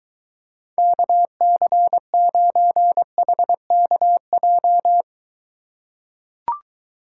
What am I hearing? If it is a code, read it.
Morse code, heard as a keyed tone, about 22 wpm, 700 Hz. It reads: KC9HKJ